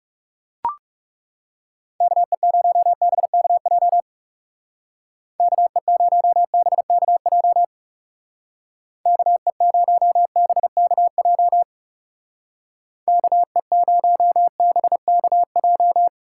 Morse code, heard as a keyed tone, about 36 wpm, 700 Hz. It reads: KE0BKJ KE0BKJ KE0BKJ KE0BKJ
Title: Speed Racing - USA 2x3 Call Signs 30wpm